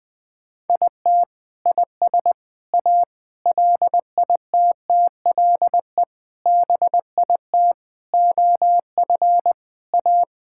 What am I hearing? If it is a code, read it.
Morse code, heard as a keyed tone, about 20 wpm, 700 Hz. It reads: IT IS A LITTLE BIT OF A
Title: Sentences with 5-Word Phrases - 20wpm